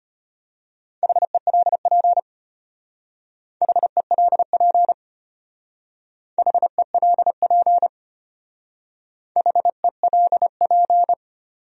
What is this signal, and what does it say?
Morse code, heard as a keyed tone, about 33 wpm, 700 Hz. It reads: HELP HELP HELP HELP